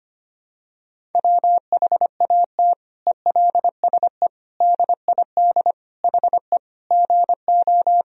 Morse code, heard as a keyed tone, about 25 wpm, 700 Hz. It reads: WHAT ELSE DID HE GO
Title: Sentences from Top 1000 Words - Review Part 1 - 25wpm